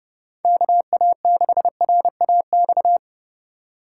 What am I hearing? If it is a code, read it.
Morse code, heard as a keyed tone, about 31 wpm, 700 Hz. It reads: KA6RAX